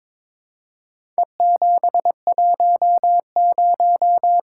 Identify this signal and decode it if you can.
Morse code, heard as a keyed tone, about 22 wpm, 700 Hz. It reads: E710
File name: practice / 22wpm / Four Letter-Number Combinations - 22wpm